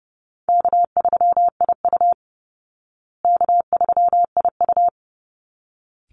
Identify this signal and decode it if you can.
Morse code, heard as a keyed tone, about 30 wpm, 700 Hz. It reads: K3IU K3IU
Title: Easy Call Signs Repeated 2x - 30wpm